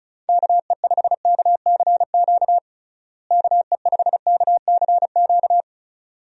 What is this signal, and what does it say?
Morse code, heard as a keyed tone, about 34 wpm, 700 Hz. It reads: KE5KCQ KE5KCQ
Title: USA Call Signs - 6 Characters - Repeated 2x - 35wpm